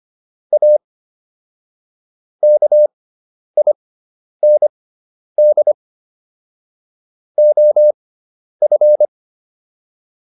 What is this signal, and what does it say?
Morse code, heard as a keyed tone, about 25 wpm, spaced as for 10 wpm, 600 Hz. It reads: A KIND OF